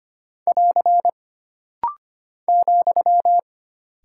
Morse code, heard as a keyed tone, about 25 wpm, 700 Hz. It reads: <AR> ,
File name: Single Punctuation-Prosign - Rapid-Fire - 25wpm